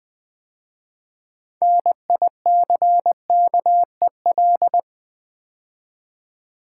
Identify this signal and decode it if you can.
Morse code, heard as a keyed tone, about 20 wpm, 700 Hz. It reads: NICKEL
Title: 6 Letter Words - First 3 Letters Given - 20wpm